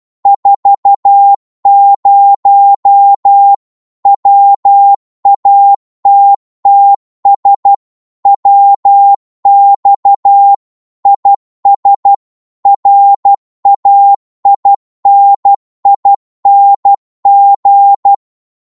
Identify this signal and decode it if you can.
Morse code, heard as a keyed tone, about 12 wpm, 800 Hz. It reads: 40 WATTS WX IS RAINING
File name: Morse / qso-12wpm